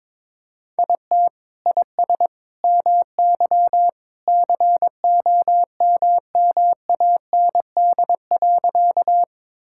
Morse code, heard as a keyed tone, about 21 wpm, 700 Hz. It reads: IT IS MY COMMAND.
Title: Sentences from Top 1600 Words 22wpm